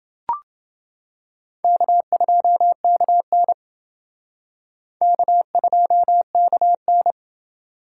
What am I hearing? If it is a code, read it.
Morse code, heard as a keyed tone, about 29 wpm, 700 Hz. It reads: K2KN K2KN